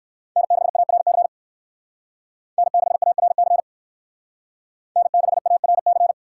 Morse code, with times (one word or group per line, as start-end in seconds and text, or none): 0.4-1.3 N6ARC
2.6-3.6 N6ARC
5.0-6.1 N6ARC